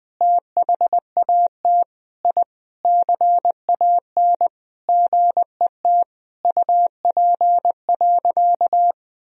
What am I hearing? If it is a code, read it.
Morse code, heard as a keyed tone, about 20 wpm, 700 Hz. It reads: THAT I CAN GET UP.
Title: Sentences from Top 1000 Words - Review Part 3 - 20wpm